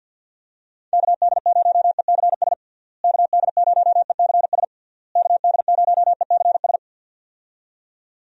Morse code, heard as a keyed tone, about 52 wpm, 700 Hz. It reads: KD0ECS KD0ECS KD0ECS
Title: USA Call Signs - 6 Characters - Repeated 3x - 50wpm